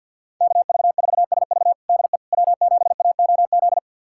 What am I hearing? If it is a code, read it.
Morse code, heard as a keyed tone, about 47 wpm, 690 Hz. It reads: KV4SV DE W7AOZ
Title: USA Callsign DE Callsign - 5 Characters - 50wpm